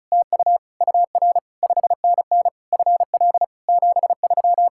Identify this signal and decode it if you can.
Morse code, heard as a keyed tone, about 33 wpm, 700 Hz. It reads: TU UR 5NN FL 73
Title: Speed-Racing Common QSO Phrases - 30wpm